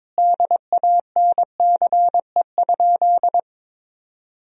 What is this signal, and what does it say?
Morse code, heard as a keyed tone, about 22 wpm, 700 Hz. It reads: DANCE?
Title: Sentences from Top 1000 Words - Review Part 4 - 22wpm